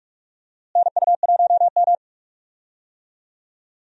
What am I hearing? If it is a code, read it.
Morse code, heard as a keyed tone, about 46 wpm, 700 Hz. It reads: NU1M
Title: USA Call Signs - 4 Characters - Repeated 3x - 45wpm